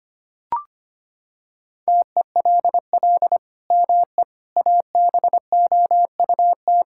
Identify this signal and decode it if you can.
Morse code, heard as a keyed tone, about 25 wpm, 700 Hz. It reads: TELL ME ABOUT
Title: Sentences from Top 1600 Words 25wpm